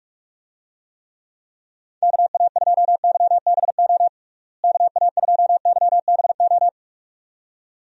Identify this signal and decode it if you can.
Morse code, heard as a keyed tone, about 46 wpm, 700 Hz. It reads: KA2YBO KA2YBO
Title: USA Call Signs - 6 Characters - Repeated 2x - 45wpm